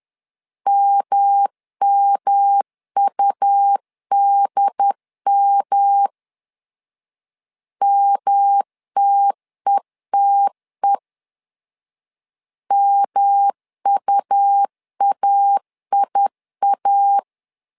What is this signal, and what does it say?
Morse code, heard as a keyed tone, about 10 wpm, 800 Hz. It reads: MMUDM MTETE MUAIA
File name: LEZIONI / 3 MWRO / 3 M 10wpm 1828